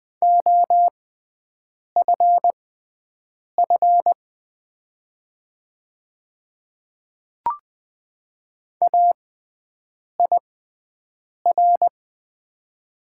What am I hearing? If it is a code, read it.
Morse code, heard as a keyed tone, about 20 wpm, spaced as for 7 wpm, 700 Hz. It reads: OFF AIR
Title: 6x Character Spacing - Top 200 Words - 20wpm